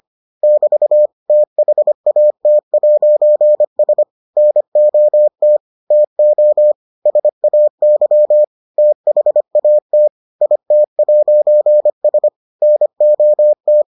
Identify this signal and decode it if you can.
Morse code, heard as a keyed tone, about 25 wpm, 600 Hz. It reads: <BT> THAT'S NOT TO SAY THAT IT'S NOT